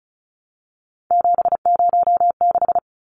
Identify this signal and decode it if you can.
Morse code, heard as a keyed tone, about 34 wpm, 700 Hz. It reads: 706
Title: Numbers - 3-Digit 35wpm